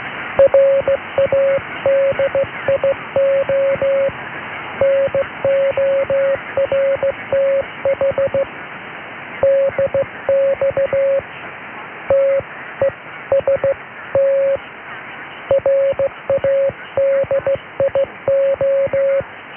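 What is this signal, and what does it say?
Morse code, heard as a keyed tone, about 16 wpm, 550 Hz. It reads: RADIO NORTH DX TEST RADIO
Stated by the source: Morse heard during a Radio North DX test on 846 kHz mediumwave, logged in transatlantic DX logs from SW Scotland.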